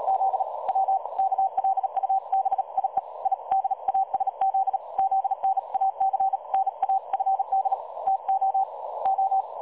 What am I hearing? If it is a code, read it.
Morse code, heard as a keyed tone, about 39 wpm, 770 Hz. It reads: Z W KLUBIE I DASZ ZNAC NA FD TO O
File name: dlazaawansowanych / nagraniazpasma / websdr recording 3565.0kHz